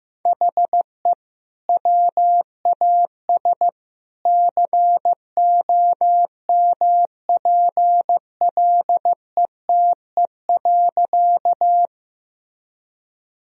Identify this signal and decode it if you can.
Morse code, heard as a keyed tone, about 15 wpm, 700 Hz. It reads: HE WAS COMPLETE.
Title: Sentences from Top 1000 Words - Review Part 2 - 15wpm